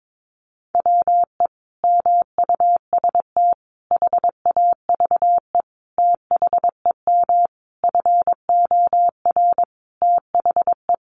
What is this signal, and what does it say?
Morse code, heard as a keyed tone, about 22 wpm, 700 Hz. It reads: WE MUST HAVE THEM FOR THE
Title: Sentences from Top 1200 Words 22wpm